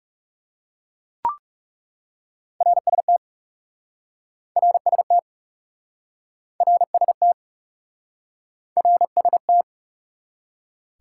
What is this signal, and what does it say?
Morse code, heard as a keyed tone, about 40 wpm, 700 Hz. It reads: RST RST RST RST